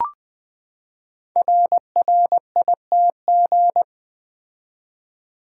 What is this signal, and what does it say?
Morse code, heard as a keyed tone, about 20 wpm, 700 Hz. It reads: RRITG
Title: Code Groups - 5 Letter:Numbers - 20wpm